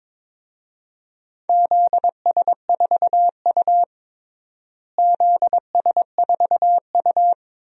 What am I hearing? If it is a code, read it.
Morse code, heard as a keyed tone, about 22 wpm, 700 Hz. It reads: ZS4U ZS4U
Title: DXCC-100 Call Signs - 4 Characters - Repeated 2x - 22wpm